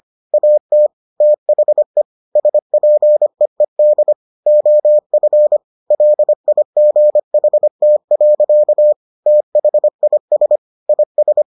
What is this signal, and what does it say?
Morse code, heard as a keyed tone, about 25 wpm, 600 Hz. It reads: AT THE SPEED OF LIGHT. THIS IS